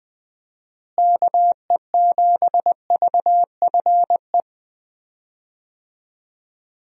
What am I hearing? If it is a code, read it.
Morse code, heard as a keyed tone, about 20 wpm, 700 Hz. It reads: KE7VFE
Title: USA Call Signs - 6 Characters - Repeated 3x - 20wpm